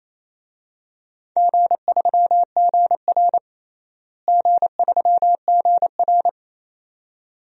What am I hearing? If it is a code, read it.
Morse code, heard as a keyed tone, about 28 wpm, 700 Hz. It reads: G3GR G3GR
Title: DXCC-100 Call Signs - 4 Characters - Repeated 2x - 28wpm